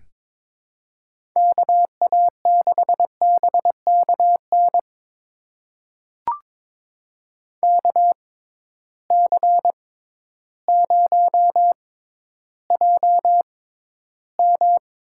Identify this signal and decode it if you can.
Morse code, heard as a keyed tone, about 22 wpm, spaced as for 12 wpm, 700 Hz. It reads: KA6BKN KC0JM
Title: USA Call Signs - 6 Characters - 6x Character Spacing - 22wpm